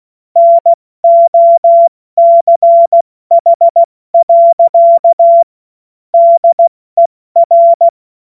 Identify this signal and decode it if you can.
Morse code, heard as a keyed tone, about 16 wpm, 680 Hz. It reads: NOCH. DER
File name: Morsekurs / Witze / Witz 16. Februar 2021, 70 F80